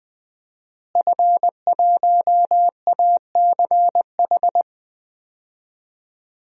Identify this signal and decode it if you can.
Morse code, heard as a keyed tone, about 20 wpm, 700 Hz. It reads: F1ACH
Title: DXCC-100 Call Signs - 5 Characters - Repeated 2x - 20wpm